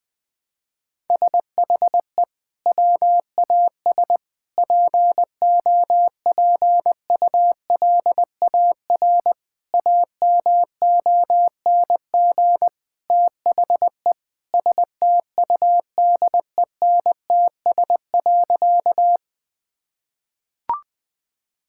Morse code, heard as a keyed tone, about 20 wpm, 700 Hz. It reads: SHE WAS POPULAR AMONG THE STUDENTS.
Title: Sentences with 2-Word Phrases - 20wpm